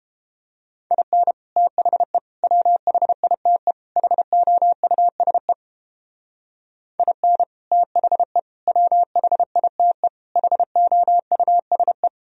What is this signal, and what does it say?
Morse code, heard as a keyed tone, about 32 wpm, 700 Hz. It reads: IN THE WHITE HOUSE IN THE WHITE HOUSE